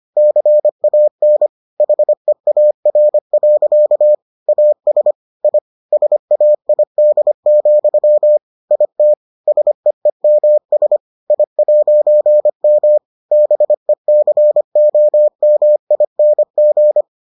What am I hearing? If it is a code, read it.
Morse code, heard as a keyed tone, about 25 wpm, 600 Hz. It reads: CAN HEAR. AS I SAID, IT SEEMS I'M BECOMING